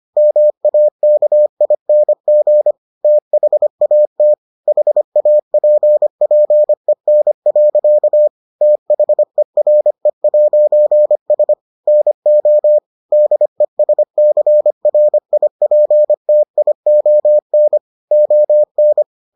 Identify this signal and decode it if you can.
Morse code, heard as a keyed tone, about 25 wpm, 600 Hz. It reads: MAKING THAT HAPPEN. THERE'S NO DESCRIPTION ON